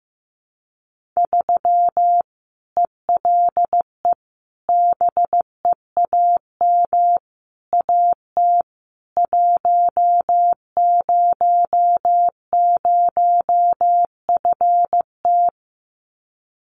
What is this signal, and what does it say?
Morse code, heard as a keyed tone, about 15 wpm, 700 Hz. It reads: 3 ELE BEAM AT 100FT
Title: Common QSO Phrases 15wpm